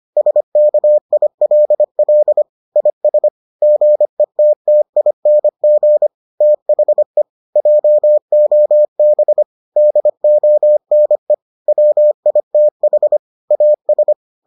Morse code, heard as a keyed tone, about 25 wpm, 600 Hz. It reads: SKILL IS GETTING THE JOB DONE WITH AS